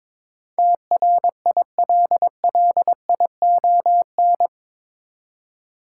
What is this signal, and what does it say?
Morse code, heard as a keyed tone, about 21 wpm, 700 Hz. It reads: TRILLION
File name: practice / 22wpm / Eight Letter Words - 22wpm